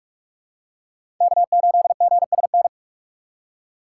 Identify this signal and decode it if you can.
Morse code, heard as a keyed tone, about 44 wpm, 700 Hz. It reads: K8GSN
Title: USA Call Signs - 5 Characters - 6x Character Spacing - 45wpm